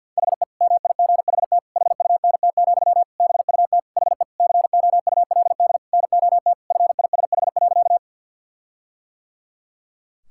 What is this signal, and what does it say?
Morse code, heard as a keyed tone, about 47 wpm, 700 Hz. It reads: HE MIGHT HUNT, BUT HE COULD NOT FISH.
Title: Sentences from Top 1600 Words 50wpm